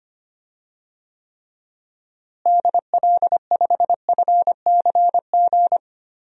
Morse code, heard as a keyed tone, about 25 wpm, 700 Hz. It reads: DL5FCG
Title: DXCC-100 Call Signs - 6 Characters - Repeated 2x - 25wpm